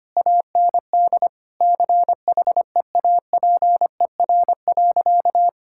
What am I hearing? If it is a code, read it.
Morse code, heard as a keyed tone, about 25 wpm, 700 Hz. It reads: AND CHEAPER.
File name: practice / 25wpm / Difficult Sentences with Hints - 25wpm